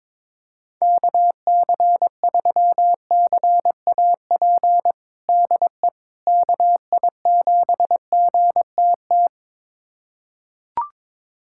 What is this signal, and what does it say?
Morse code, heard as a keyed tone, about 22 wpm, 700 Hz. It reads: KC3CAP DE KI7GTT
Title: USA Callsign DE Callsign 22wpm